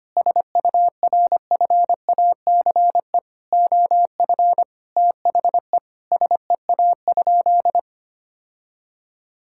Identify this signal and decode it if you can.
Morse code, heard as a keyed tone, about 25 wpm, 700 Hz. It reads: SURFACE OF THE SEA?
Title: Sentences from Top 1000 Words - Review Part 5 - 25wpm